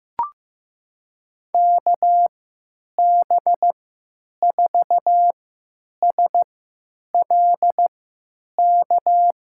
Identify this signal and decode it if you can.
Morse code, heard as a keyed tone, about 15 wpm, spaced as for 9 wpm, 700 Hz. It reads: KB4SLK
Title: USA Call Signs - 6 Characters - 3x Character Spacing - 15wpm